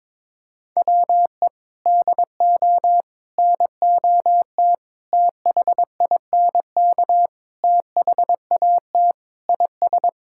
Morse code, heard as a keyed tone, about 22 wpm, 700 Hz. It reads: WE DO NOT THINK THAT IS
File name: practice / 22wpm / Sentences with 4-Word Phrases - 22wpm